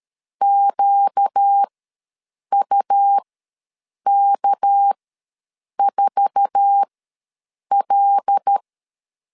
Morse code, heard as a keyed tone, about 13 wpm, spaced as for 7 wpm, 800 Hz. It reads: QUK4L
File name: LEZIONI / 8 YJ45 / 8 YJ4 ws 12wpm 3249